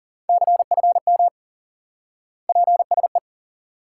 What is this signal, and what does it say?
Morse code, heard as a keyed tone, about 40 wpm, 700 Hz. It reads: CFM PSE
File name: IQE Course 3 - Lesson 89 - YAGI - Two Elements - 40wpm